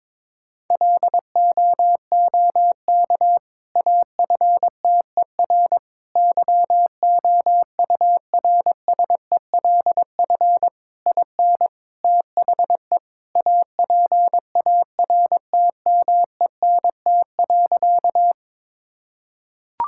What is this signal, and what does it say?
Morse code, heard as a keyed tone, about 22 wpm, 700 Hz. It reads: LOOK AFTER YOURSELF IN THE APARTMENT.
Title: Sentences from Top 1000 Words - Review Part 6 - 22wpm